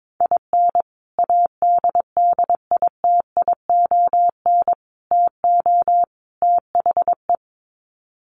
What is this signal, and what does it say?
Morse code, heard as a keyed tone, about 22 wpm, 700 Hz. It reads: IN ADDITION TO THE
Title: Sets of 4 Words 22wpm